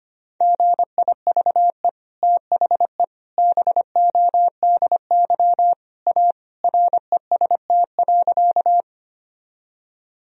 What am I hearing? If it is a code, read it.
Morse code, heard as a keyed tone, about 25 wpm, 700 Hz. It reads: GIVE THE BODY A REST.
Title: Sentences from Top 1000 Words - Review Part 2 - 25wpm